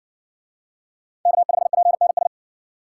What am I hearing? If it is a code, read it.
Morse code, heard as a keyed tone, about 60 wpm, 700 Hz. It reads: K5PNS